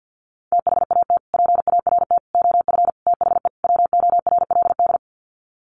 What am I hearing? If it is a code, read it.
Morse code, heard as a keyed tone, about 48 wpm, 700 Hz. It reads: THAT PART OF THE WORLD
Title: Sets of 5 Words Encore2 50wpm